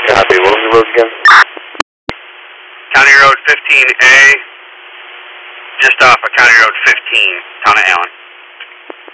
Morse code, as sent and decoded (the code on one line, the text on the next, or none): none